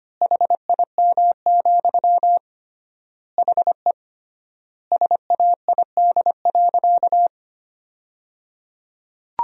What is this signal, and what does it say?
Morse code, heard as a keyed tone, about 25 wpm, 700 Hz. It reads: HIM, HE SAID.
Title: 3x Word-Spacing - Sentences from Top 300 Words - 25wpm